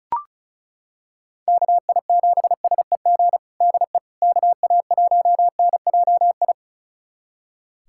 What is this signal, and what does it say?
Morse code, heard as a keyed tone, about 36 wpm, 700 Hz. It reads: KI7SEG DE KA1NJI
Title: USA Callsign DE Callsign - 6 Characters - 35wpm